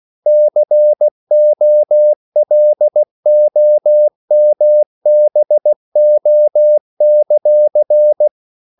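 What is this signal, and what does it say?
Morse code, heard as a keyed tone, about 16 wpm, 600 Hz. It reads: COLOMBO;
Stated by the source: CW practice of national capitals at 16 words per minute, from ON6ZQ.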